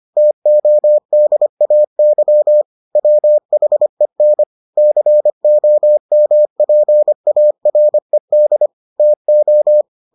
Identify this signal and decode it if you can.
Morse code, heard as a keyed tone, about 25 wpm, 600 Hz. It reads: TODAY WHEN COMPARED TO